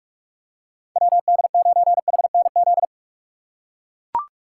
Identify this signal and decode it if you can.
Morse code, heard as a keyed tone, about 43 wpm, 700 Hz. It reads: WD9HNZ